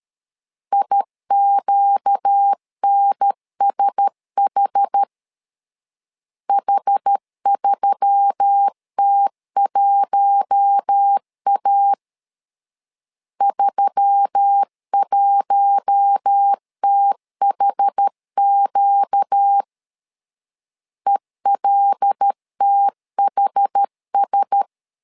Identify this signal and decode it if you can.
Morse code, heard as a keyed tone, about 13 wpm, 800 Hz. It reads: IQNSH H3T1A 31THQ ELTHS
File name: LEZIONI / 5 LBVC / 5 L 12wpm 1913